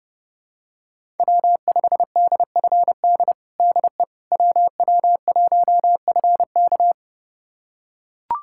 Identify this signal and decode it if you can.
Morse code, heard as a keyed tone, about 32 wpm, 700 Hz. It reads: W5DFD DE WW1FK
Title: USA Callsign DE Callsign - 5 Characters - 30wpm